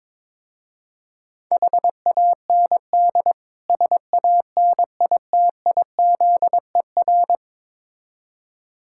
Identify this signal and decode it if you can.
Morse code, heard as a keyed tone, about 22 wpm, 700 Hz. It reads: HAND SANITIZER
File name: practice / 22wpm / Difficult Words with Hints - 22wpm